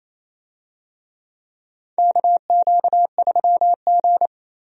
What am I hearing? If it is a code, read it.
Morse code, heard as a keyed tone, about 27 wpm, 700 Hz. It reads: KQ3G
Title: USA Call Signs - 4 Characters - 28wpm